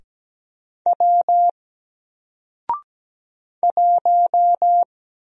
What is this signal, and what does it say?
Morse code, heard as a keyed tone, about 17 wpm, 700 Hz. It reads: W 1
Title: Single Character (Letter:Number:Punctuation:Prosign) Alt 17wpm